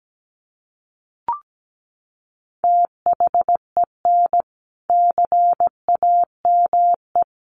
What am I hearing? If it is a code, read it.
Morse code, heard as a keyed tone, about 17 wpm, 700 Hz. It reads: THEN CAME
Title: Sentences from Top 1500 Words 17wpm